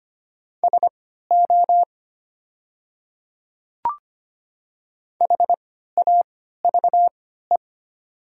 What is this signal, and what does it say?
Morse code, heard as a keyed tone, about 25 wpm, spaced as for 14 wpm, 700 Hz. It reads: SO HAVE
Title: 3x Character Spacing - Top 100 Words - 25wpm